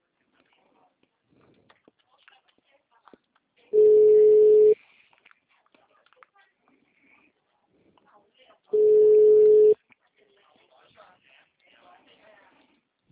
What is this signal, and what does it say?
Ringback tone (Europe, 425 Hz), with speech.